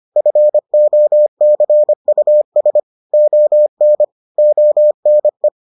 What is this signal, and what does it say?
Morse code, heard as a keyed tone, about 25 wpm, 600 Hz. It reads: FOCUS ON ONE